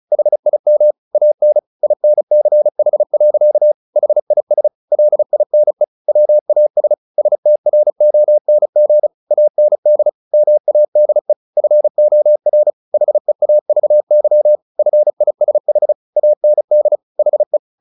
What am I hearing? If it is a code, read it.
Morse code, heard as a keyed tone, about 34 wpm, 600 Hz. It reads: HIM AN INCH. HIS LINE WAS STRONG AND MADE FOR HEAVY FISH AND HE
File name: the old man and the sea / the old man and the sea 1 35wpm